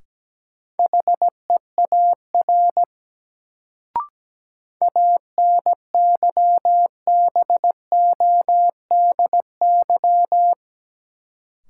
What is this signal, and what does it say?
Morse code, heard as a keyed tone, about 17 wpm, 700 Hz. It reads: HEAR ANYBODY
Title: Sets of 5 Words - New Word Drill - 17wpm